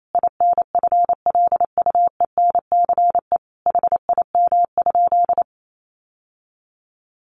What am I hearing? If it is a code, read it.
Morse code, heard as a keyed tone, about 28 wpm, 700 Hz. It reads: INFLUENCE HIM?
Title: Sentences from Top 1100 Words 28wpm